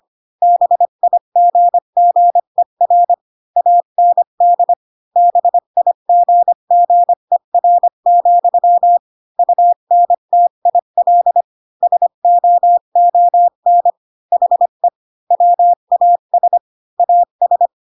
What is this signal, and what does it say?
Morse code, heard as a keyed tone, about 25 wpm, 700 Hz. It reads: BIGGER AND BIGGER, UNTIL SOON HE WAS AS
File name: Pinocchio / Pinocchio ch5 20 25 ik1hge